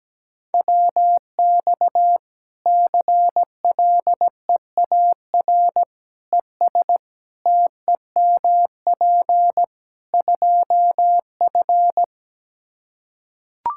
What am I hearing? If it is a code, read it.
Morse code, heard as a keyed tone, about 17 wpm, 700 Hz. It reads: WX CLEAR ES TEMP 2F
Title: Practice QSO - Exchange 2 - Info - 17wpm